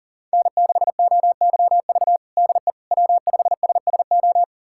Morse code, heard as a keyed tone, about 37 wpm, 700 Hz. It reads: N6OYV DE W5SSO
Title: USA Callsign DE Callsign - 5 Characters - 40wpm